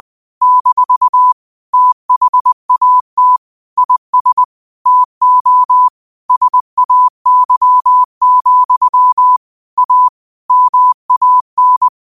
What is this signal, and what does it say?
Morse code, heard as a keyed tone, about 20 wpm, 1000 Hz. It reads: - THAT IS TO SAY, A MAN